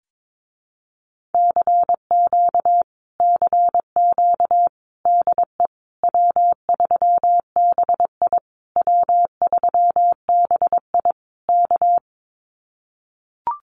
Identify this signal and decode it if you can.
Morse code, heard as a keyed tone, about 22 wpm, 700 Hz. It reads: CQ CQ DE W3BI W3BI K
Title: Common QSO Phrases Encore 22wpm